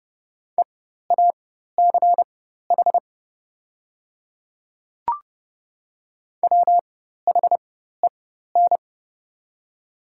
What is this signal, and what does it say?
Morse code, heard as a keyed tone, about 30 wpm, spaced as for 14 wpm, 700 Hz. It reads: EACH WHEN
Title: 4x Character Spacing - Top 100 Words - 30wpm